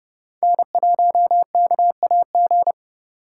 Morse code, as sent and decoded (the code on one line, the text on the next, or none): -. .---- -.- .- --.
N1KAG